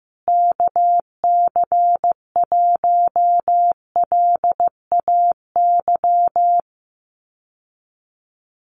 Morse code, heard as a keyed tone, about 15 wpm, 700 Hz. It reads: KC1LAY
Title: USA Call Signs Repeated 2x 15wpm